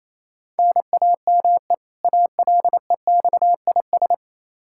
Morse code, heard as a keyed tone, about 28 wpm, 700 Hz. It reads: NAME ALEXIS
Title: Practice QSO - Exchange 1 - Beginning - 28wpm